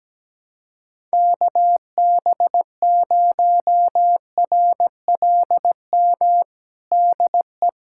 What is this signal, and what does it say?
Morse code, heard as a keyed tone, about 17 wpm, 700 Hz. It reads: KB0RLM DE
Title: USA Callsign DE Callsign - 6 Characters - 17wpm